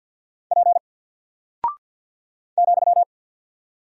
Morse code, heard as a keyed tone, about 48 wpm, 700 Hz. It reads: P ,
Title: Single Character (Letter:Number:Punctuation:Prosign) 50wpm